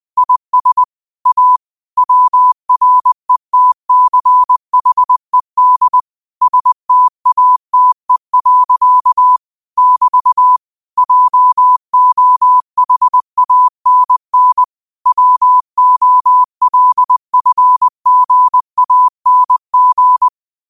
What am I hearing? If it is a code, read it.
Morse code, heard as a keyed tone, about 20 wpm, 1000 Hz. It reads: IS A WRETCHED STATE. <BT> JOHANN WOLFGANG
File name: qotdfiles / 20wpm